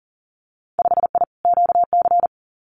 Morse code, heard as a keyed tone, about 40 wpm, 700 Hz. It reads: 5I QC
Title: Winter Field Day Exchanges Encore 40wpm